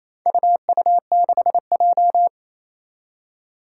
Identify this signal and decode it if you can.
Morse code, heard as a keyed tone, about 28 wpm, 700 Hz. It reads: UU6J
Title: Four Challenging Letter-Number Combinations 28wpm